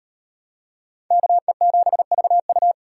Morse code, heard as a keyed tone, about 35 wpm, 700 Hz. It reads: KE7VU